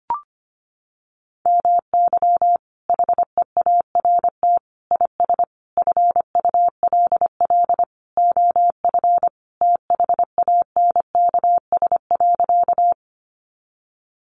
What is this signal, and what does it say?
Morse code, heard as a keyed tone, about 25 wpm, 700 Hz. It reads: MY HEART IS FULL OF THANKS.
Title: Sentences from Top 1200 Words 25wpm